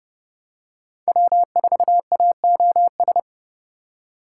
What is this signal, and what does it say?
Morse code, heard as a keyed tone, about 29 wpm, 700 Hz. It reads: W4AOS